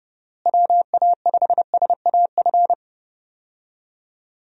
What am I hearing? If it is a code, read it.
Morse code, heard as a keyed tone, about 32 wpm, 700 Hz. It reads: WA5SAF
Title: USA Call Signs - 6 Characters - Repeated 2x - 30wpm